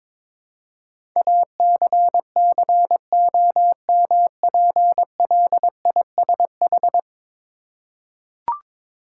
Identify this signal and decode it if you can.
Morse code, heard as a keyed tone, about 21 wpm, 700 Hz. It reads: ACCOMPLISH
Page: morse-code-ninja-practice-22wpm